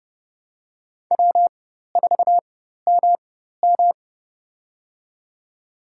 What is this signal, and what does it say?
Morse code, heard as a keyed tone, about 32 wpm, spaced as for 14 wpm, 700 Hz. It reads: W4MM
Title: USA Call Signs - 4 Characters - 4x Character Spacing - 30wpm